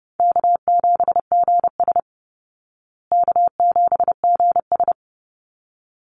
Morse code, heard as a keyed tone, about 30 wpm, 700 Hz. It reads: K7GS K7GS
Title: Call Signs Repeated 3x - 30wpm